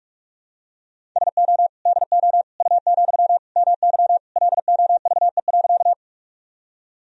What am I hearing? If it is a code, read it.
Morse code, heard as a keyed tone, about 46 wpm, 700 Hz. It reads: SO DO U, MY LOVE.
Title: Sentences from Top 400 Words and Abbreviations 45wpm